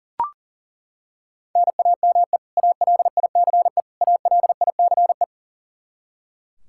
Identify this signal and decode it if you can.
Morse code, heard as a keyed tone, about 40 wpm, 700 Hz. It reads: NAME ALICE ALICE
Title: Practice QSO - Exchange 1 - Beginning - 40wpm